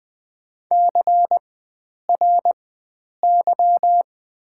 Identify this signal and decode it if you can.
Morse code, heard as a keyed tone, about 20 wpm, spaced as for 9 wpm, 700 Hz. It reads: CRY
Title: 4x Character Spacing - Top 500 Words - 20wpm